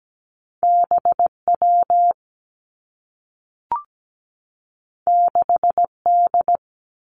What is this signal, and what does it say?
Morse code, heard as a keyed tone, about 17 wpm, 700 Hz. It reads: BW 6D
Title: Two Challenging Letter-Number Combinations 17wpm